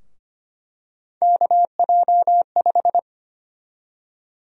Morse code, heard as a keyed tone, about 25 wpm, 700 Hz. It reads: KJ5
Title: USA Call Sign Prefixes - Repeated 2x - 25wpm